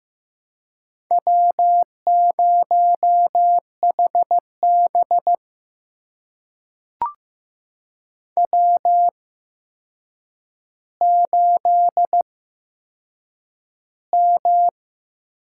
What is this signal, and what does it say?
Morse code, heard as a keyed tone, about 15 wpm, spaced as for 7 wpm, 700 Hz. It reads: W0HB W8M
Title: USA Call Signs - 4 Characters - 8x Character Spacing - 15wpm